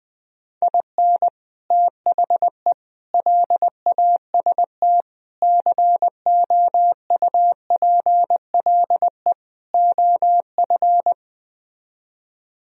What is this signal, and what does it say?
Morse code, heard as a keyed tone, about 20 wpm, 700 Hz. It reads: IN THE LAST COUPLE OF
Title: Speed-Racing - Sets of 5 Words 20wpm